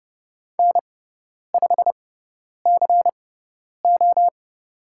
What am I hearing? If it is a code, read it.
Morse code, heard as a keyed tone, about 30 wpm, spaced as for 10 wpm, 700 Hz. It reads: N5CO
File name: Lesson 16g - Letter c - Call Signs 30wpm